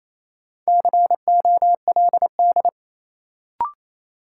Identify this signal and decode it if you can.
Morse code, heard as a keyed tone, about 28 wpm, 700 Hz. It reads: COLD
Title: Sets of 5 Words - New Word Drill - 28wpm